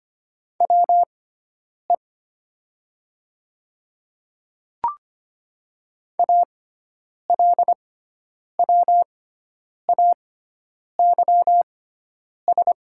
Morse code, heard as a keyed tone, about 25 wpm, spaced as for 9 wpm, 690 Hz. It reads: WE ALWAYS
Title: 6x Character Spacing - Top 500 Words - 25wpm